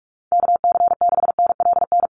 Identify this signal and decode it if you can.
Morse code, heard as a keyed tone, about 44 wpm, 700 Hz. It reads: KC6NLN
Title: USA Call Signs Repeated 3x 45wpm